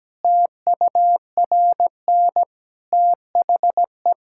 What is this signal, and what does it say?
Morse code, heard as a keyed tone, about 17 wpm, 700 Hz. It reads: TURN THE
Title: Sentences from Top 1000 Words - Review Part 5 - 17wpm